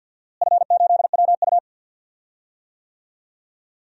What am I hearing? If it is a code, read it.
Morse code, heard as a keyed tone, about 53 wpm, 700 Hz. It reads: F8WU
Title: DXCC-100 Call Signs - 4 Characters - Repeated 2x - 50wpm